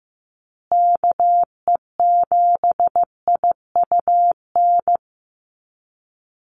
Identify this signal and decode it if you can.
Morse code, heard as a keyed tone, about 15 wpm, 700 Hz. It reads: KE7IUN
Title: USA Call Signs Repeated 3x 15wpm